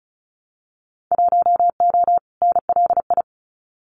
Morse code, heard as a keyed tone, about 34 wpm, 700 Hz. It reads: 1O NLI
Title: Winter Field Day Exchanges 35wpm